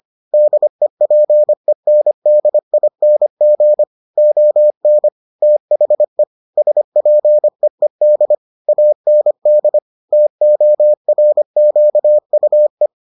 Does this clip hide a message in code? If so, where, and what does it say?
Yes: Morse code from 0.3 to 12.9 s, heard as a keyed tone, about 25 wpm, 600 Hz. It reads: DEPENDING ON THE SPEED AND TORQUE